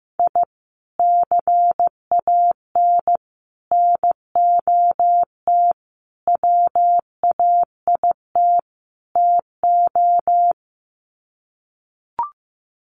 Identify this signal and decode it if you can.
Morse code, heard as a keyed tone, about 15 wpm, 700 Hz. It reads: I CAN NOT WAIT TO
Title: Sets of 5 Words Encore 15wpm